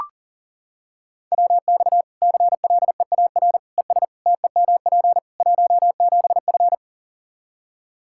Morse code, heard as a keyed tone, about 40 wpm, 700 Hz. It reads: WX CLEAR ES TEMP 17F
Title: Practice QSO - Exchange 2 - Info - 40wpm